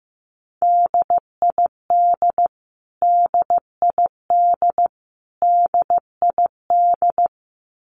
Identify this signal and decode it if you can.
Morse code, heard as a keyed tone, about 15 wpm, 700 Hz. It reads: DID DID DID
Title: Top 500 Words Repeated 15wpm